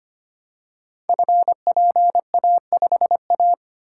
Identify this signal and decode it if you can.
Morse code, heard as a keyed tone, about 25 wpm, 700 Hz. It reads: FPA5A
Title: Code Groups - 5 Letter:Numbers - 25wpm